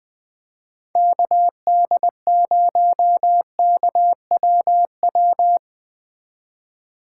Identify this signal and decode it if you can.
Morse code, heard as a keyed tone, about 19 wpm, 700 Hz. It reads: KD0KWW
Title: Speed Racing - USA 2x3 Call Signs 17wpm